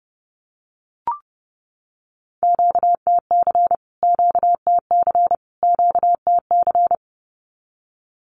Morse code, heard as a keyed tone, about 30 wpm, 700 Hz. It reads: QTC QTC QTC
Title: Q-Codes Repeated 3x 30wpm